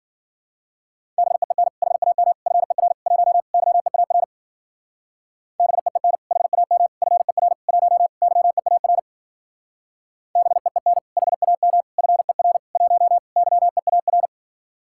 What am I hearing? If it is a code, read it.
Morse code, heard as a keyed tone, about 50 wpm, 700 Hz. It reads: BEEN HAM FER 1 YEAR BEEN HAM FER 1 YEAR BEEN HAM FER 1 YEAR